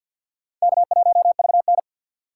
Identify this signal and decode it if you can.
Morse code, heard as a keyed tone, about 45 wpm, 690 Hz. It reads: K1VN
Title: USA Call Signs - 4 Characters - 6x Character Spacing - 50wpm